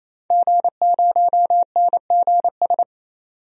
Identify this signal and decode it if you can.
Morse code, heard as a keyed tone, about 28 wpm, 700 Hz. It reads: G0NGS
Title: DXCC-100 Call Signs - 5 Characters - 28wpm